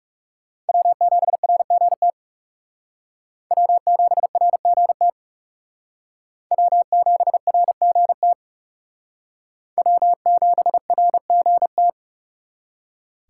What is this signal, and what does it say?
Morse code, heard as a keyed tone, about 36 wpm, 700 Hz. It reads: W7RGT W7RGT W7RGT W7RGT
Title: Speed Racing - USA 1x3 Call Signs 30wpm